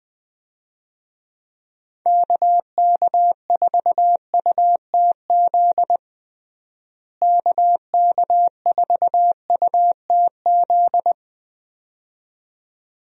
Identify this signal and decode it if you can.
Morse code, heard as a keyed tone, about 20 wpm, 700 Hz. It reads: KK4UTZ KK4UTZ
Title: USA Call Signs - 6 Characters - Repeated 2x - 20wpm